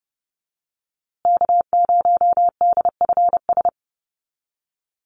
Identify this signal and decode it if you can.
Morse code, heard as a keyed tone, about 28 wpm, 700 Hz. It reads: K0DFS
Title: USA Call Signs Repeated 3x 30wpm